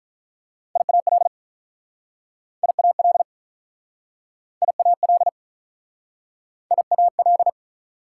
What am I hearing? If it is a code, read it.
Morse code, heard as a keyed tone, about 48 wpm, 700 Hz. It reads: IAL IAL IAL IAL